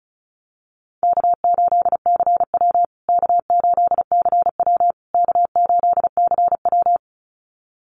Morse code, heard as a keyed tone, about 34 wpm, 700 Hz. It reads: K8CW K8CW K8CW
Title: USA Call Signs Repeated 3x 35wpm